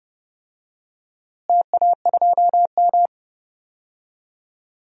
Morse code, heard as a keyed tone, about 31 wpm, 700 Hz. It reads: TA2M
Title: DXCC-100 Call Signs - 4 Characters - Repeated 3x - 30wpm